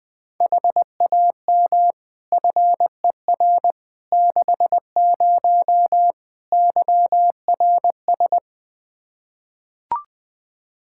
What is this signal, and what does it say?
Morse code, heard as a keyed tone, about 20 wpm, 700 Hz. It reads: HAM FER 60 YRS
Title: Practice QSO - Exchange 3 - More Info - 20wpm